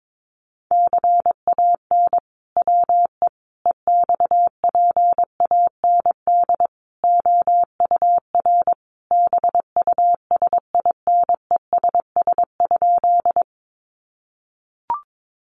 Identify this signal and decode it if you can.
Morse code, heard as a keyed tone, about 22 wpm, 700 Hz. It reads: CAN WE EXPAND OUR BUSINESS?
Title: Sentences from Top 1400 Words 22wpm